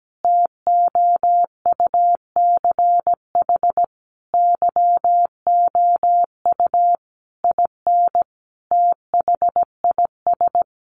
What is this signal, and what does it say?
Morse code, heard as a keyed tone, about 17 wpm, 700 Hz. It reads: TOUCH YOU IN THIS
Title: Sentences from Top 1200 Words 17wpm